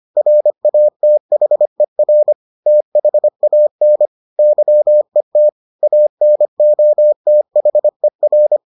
Morse code, heard as a keyed tone, about 25 wpm, 600 Hz. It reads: RATHER THAN YET ANOTHER